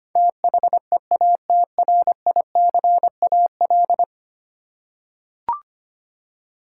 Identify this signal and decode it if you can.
Morse code, heard as a keyed tone, about 25 wpm, 700 Hz. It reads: THEATRICAL
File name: 10 Letter Words - First 3 Letters Given - 25wpm